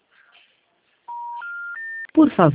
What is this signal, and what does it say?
Special information tone, with speech.